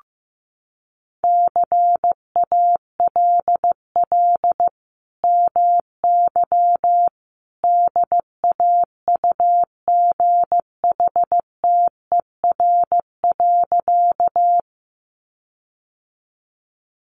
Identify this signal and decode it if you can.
Morse code, heard as a keyed tone, about 15 wpm, 700 Hz. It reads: CALL MY DAUGHTER.
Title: Sentences from Top 1500 Words 15wpm